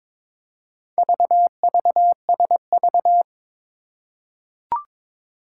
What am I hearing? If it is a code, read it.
Morse code, heard as a keyed tone, about 22 wpm, 700 Hz. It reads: VVSV
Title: Four Challenging Letter-Number Combinations 22wpm